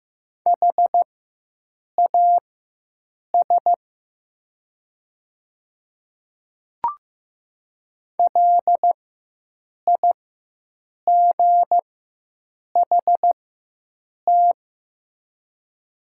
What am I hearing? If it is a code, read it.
Morse code, heard as a keyed tone, about 15 wpm, spaced as for 7 wpm, 700 Hz. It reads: HAS LIGHT
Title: 4x Character Spacing - Top 200 Words - 15wpm